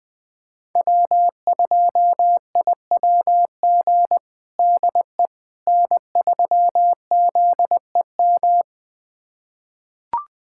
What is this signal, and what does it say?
Morse code, heard as a keyed tone, about 20 wpm, 700 Hz. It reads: W2IWG DE N3ZEM
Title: USA Callsign DE Callsign - 5 Characters - 20wpm